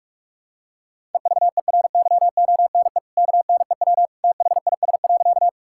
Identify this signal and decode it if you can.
Morse code, heard as a keyed tone, about 42 wpm, 700 Hz. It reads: EVERYONE KNEW THIS.
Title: Sentences from Top 1000 Words - Review Part 5 - 45wpm